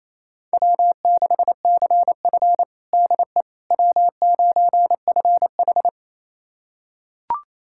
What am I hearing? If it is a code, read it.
Morse code, heard as a keyed tone, about 28 wpm, 700 Hz. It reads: W6CF DE W9FH
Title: USA Callsign DE Callsign - 4 Characters - 28wpm